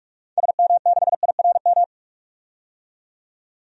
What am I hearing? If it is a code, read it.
Morse code, heard as a keyed tone, about 42 wpm, 700 Hz. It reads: SM6IRM
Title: DXCC-100 Call Signs - 6 Characters - 45wpm